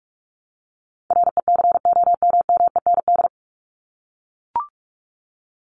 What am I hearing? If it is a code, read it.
Morse code, heard as a keyed tone, about 44 wpm, 700 Hz. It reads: RECOMMEND
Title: Top 1001-1250 Words 45wpm